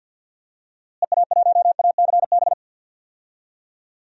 Morse code, heard as a keyed tone, about 48 wpm, 700 Hz. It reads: EA1ACB
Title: DXCC-100 Call Signs - 6 Characters - Repeated 3x - 50wpm